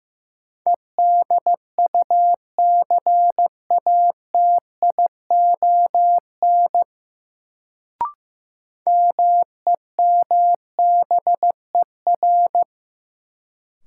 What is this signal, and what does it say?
Morse code, heard as a keyed tone, about 15 wpm, 700 Hz. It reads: EDUCATION MEMBER
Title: Sentences from Top 1100 Words - New Word Drill Alt - 15wpm